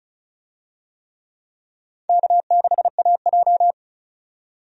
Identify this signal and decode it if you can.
Morse code, heard as a keyed tone, about 34 wpm, 700 Hz. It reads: K6AJ
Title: USA Call Signs - 4 Characters - 35wpm